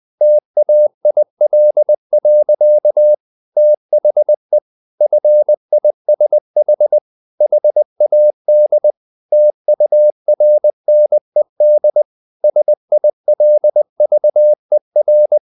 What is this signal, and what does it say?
Morse code, heard as a keyed tone, about 20 wpm, 600 Hz. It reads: TAIL. THE FISH HAD TURNED SILVER
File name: the old man and the sea part 2 20wpm